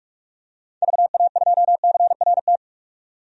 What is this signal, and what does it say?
Morse code, heard as a keyed tone, about 41 wpm, 700 Hz. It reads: VA2CRT
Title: DXCC-100 Call Signs - 6 Characters - 45wpm